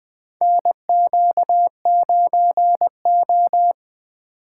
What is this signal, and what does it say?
Morse code, heard as a keyed tone, about 20 wpm, 700 Hz. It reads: NQ9O
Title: USA Call Signs - 3x Character Spacing - 20wpm